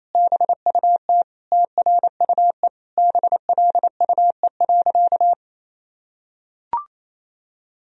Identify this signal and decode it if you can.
Morse code, heard as a keyed tone, about 28 wpm, 700 Hz. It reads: BUT TRUE BLUE.
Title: Sentences from Top 1000 Words - Review Part 3 - 28wpm